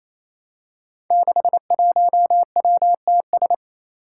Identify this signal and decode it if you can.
Morse code, heard as a keyed tone, about 28 wpm, 700 Hz. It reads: 61WTS
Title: Five Challenging Letter-Number Combinations 28wpm